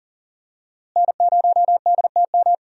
Morse code, heard as a keyed tone, about 40 wpm, 700 Hz. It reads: N0DTM
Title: USA Call Signs - 5 Characters - Repeated 3x - 40wpm